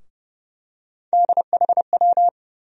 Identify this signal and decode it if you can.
Morse code, heard as a keyed tone, about 30 wpm, 700 Hz. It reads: DHW
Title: USA Call Sign Postfixes - 3 Letters 30wpm